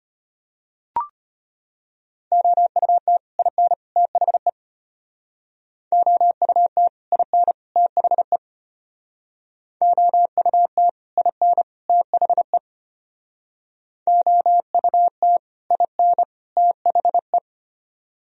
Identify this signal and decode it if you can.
Morse code, heard as a keyed tone, about 32 wpm, 700 Hz. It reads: OUT IN THE OUT IN THE OUT IN THE OUT IN THE